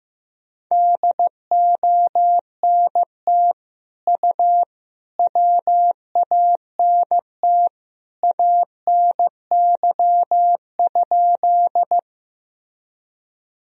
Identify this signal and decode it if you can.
Morse code, heard as a keyed tone, about 15 wpm, 700 Hz. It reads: DONT U WANT ANY?
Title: Sentences from Top 300 Words and Abbreviations 15wpm